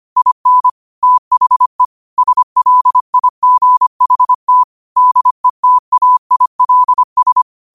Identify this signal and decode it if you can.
Morse code, heard as a keyed tone, about 25 wpm, 1000 Hz. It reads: IN THE SLIGHT DETAILS
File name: qotdfiles / 25wpm